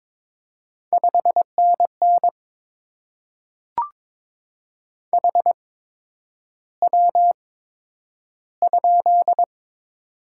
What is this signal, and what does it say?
Morse code, heard as a keyed tone, about 22 wpm, spaced as for 9 wpm, 700 Hz. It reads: 5NN HW?
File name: Top 20 QSO Elements - 8x Character Spacing - 22wpm